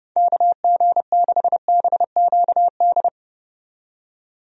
Morse code, heard as a keyed tone, about 32 wpm, 700 Hz. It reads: KG6BQD
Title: USA Call Signs - 6 Characters - 30wpm